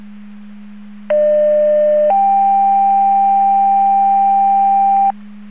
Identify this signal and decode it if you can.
Two-tone page, heard as a one-second tone, then a three-second tone, about 600 then 790 Hz. It is heard from 1.1 to 5.1 s.